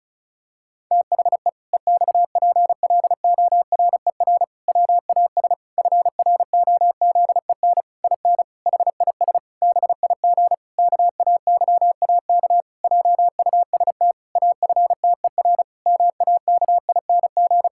Morse code, heard as a keyed tone, about 36 wpm, 700 Hz. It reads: THE EXPLORER WAS FROZEN IN HIS BIG KAYAK JUST AFTER MAKING
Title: Pangram Challenge v2 - 35wpm